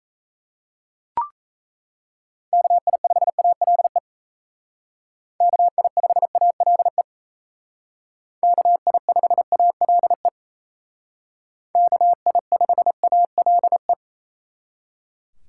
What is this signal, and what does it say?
Morse code, heard as a keyed tone, about 35 wpm, 700 Hz. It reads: KI5ALE KI5ALE KI5ALE KI5ALE
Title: Speed Racing - USA 2x3 Call Signs 28wpm